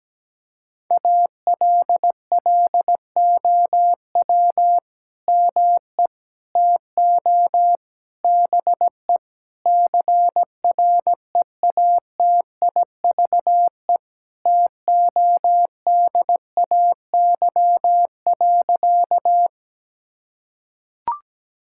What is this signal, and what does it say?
Morse code, heard as a keyed tone, about 17 wpm, 700 Hz. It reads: ALLOW ME TO BE CREATIVE TODAY.
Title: Sentences from Top 1000 Words - Review Part 5 - 17wpm